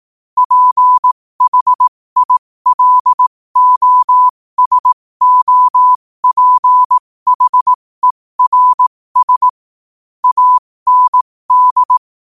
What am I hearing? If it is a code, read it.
Morse code, heard as a keyed tone, about 18 wpm, spaced as for 15 wpm, 1000 Hz. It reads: PHILOSOPHERS AND
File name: qotdfiles / 15wpm